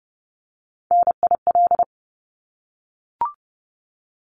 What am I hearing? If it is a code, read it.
Morse code, heard as a keyed tone, about 30 wpm, 700 Hz. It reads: NIL
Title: Abbreviations Repeated 3x - 30wpm